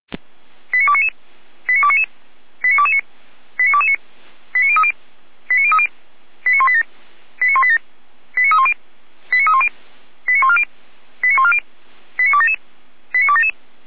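Five-tone selective call (ZVEI), heard as quick runs of five tones, tones of about 70 ms. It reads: 89299 89299 89229 89229 89030 89030 89187 89187 89310 89310 89140 89140 89277 89277
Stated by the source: ZVEI five-tone call 89075 received by SDR.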